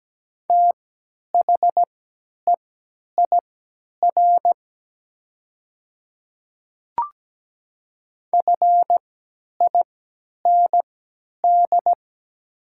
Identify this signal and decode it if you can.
Morse code, heard as a keyed tone, about 17 wpm, spaced as for 10 wpm, 700 Hz. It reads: THEIR FIND
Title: 3x Character Spacing - Top 100 Words - 17wpm